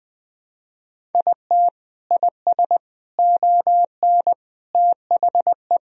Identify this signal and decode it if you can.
Morse code, heard as a keyed tone, about 20 wpm, 700 Hz. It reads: IT IS ON THE
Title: Sentences from Top 1000 Words - Review Part 1 - 20wpm